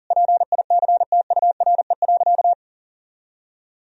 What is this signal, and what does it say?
Morse code, heard as a keyed tone, about 40 wpm, 700 Hz. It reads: PICTURE.
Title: Sentences from Top 1000 Words - Review Part 1 - 40wpm